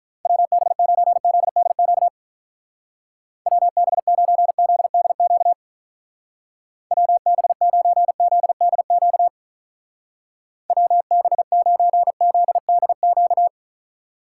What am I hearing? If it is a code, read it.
Morse code, heard as a keyed tone, about 45 wpm, 700 Hz. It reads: WB9ZDQ WB9ZDQ WB9ZDQ WB9ZDQ